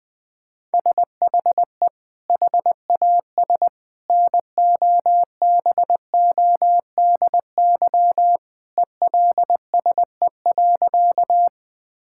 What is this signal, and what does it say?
Morse code, heard as a keyed tone, about 20 wpm, 700 Hz. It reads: SHE HAS NOBODY ELSE.
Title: Sentences from Top 1600 Words 20wpm